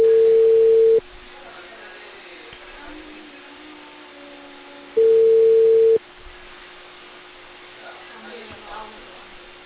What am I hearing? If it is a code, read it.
Ringback tone (China, 450 Hz), with speech.